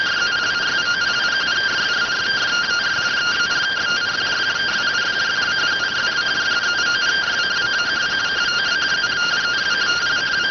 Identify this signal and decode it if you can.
Two-tone teleprinter signal, about 170 Hz shift.